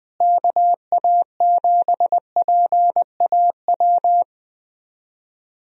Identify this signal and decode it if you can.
Morse code, heard as a keyed tone, about 20 wpm, 700 Hz. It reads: KA7PAW
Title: USA Call Signs - 6 Characters - Repeated 3x - 20wpm